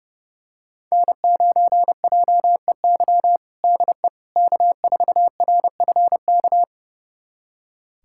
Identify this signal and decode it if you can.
Morse code, heard as a keyed tone, about 30 wpm, 700 Hz. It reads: N9JEY DE K4RFK
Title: USA Callsign DE Callsign - 5 Characters - 30wpm